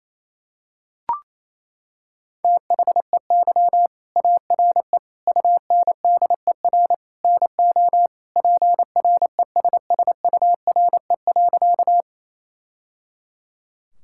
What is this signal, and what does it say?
Morse code, heard as a keyed tone, about 27 wpm, 700 Hz. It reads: THEY ARE UNDER NO PRESSURE.
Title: Sentences from Top 1000 Words - Review Part 6 - 28wpm